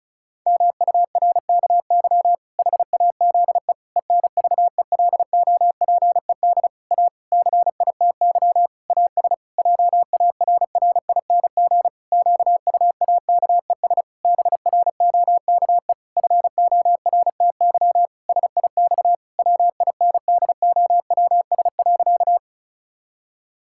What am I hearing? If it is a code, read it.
Morse code, heard as a keyed tone, about 36 wpm, 700 Hz. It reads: MURKY HAZE ENVELOPED A CITY AS JARRING QUAKES BROKE FORTY SIX WINDOWS.
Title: Pangram Challenge v2 - 35wpm